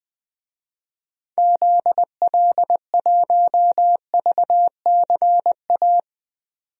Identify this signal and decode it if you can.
Morse code, heard as a keyed tone, about 21 wpm, 700 Hz. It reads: ZL1VCA
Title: DXCC-100 Call Signs - 6 Characters - 20wpm